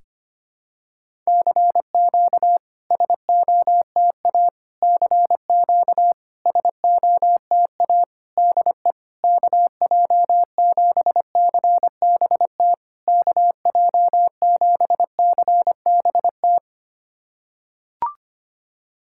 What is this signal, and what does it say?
Morse code, heard as a keyed tone, about 25 wpm, 700 Hz. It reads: CQ SOTA CQ SOTA DE KJ7CBT KJ7CBT
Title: SOTA QSO Practice 25wpm